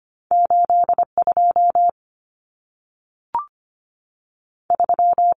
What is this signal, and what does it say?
Morse code, heard as a keyed tone, about 25 wpm, spaced as for 3 wpm, 700 Hz. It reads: 8 23
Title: Numbers - 2-Digit 25wpm